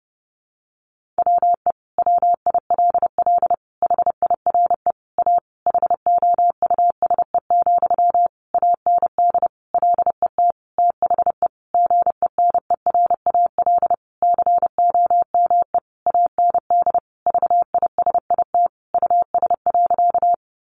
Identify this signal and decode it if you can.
Morse code, heard as a keyed tone, about 30 wpm, 700 Hz. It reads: WE WILL HIRE A HOUSE, AND LET THE GENERAL COME AND VISIT US.
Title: Sentences from Top 1400 Words 30wpm